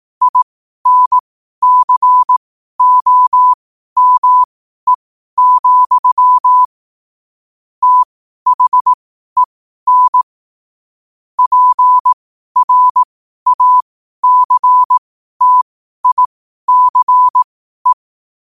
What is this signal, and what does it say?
Morse code, heard as a keyed tone, about 18 wpm, spaced as for 12 wpm, 1000 Hz. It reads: INCOME, THEN PRACTICE
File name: qotdfiles / 12wpm